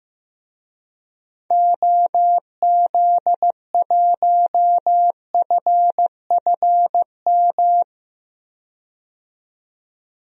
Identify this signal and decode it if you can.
Morse code, heard as a keyed tone, about 15 wpm, 700 Hz. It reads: OZ1FFM
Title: DXCC-100 Call Signs - 6 Characters - 15wpm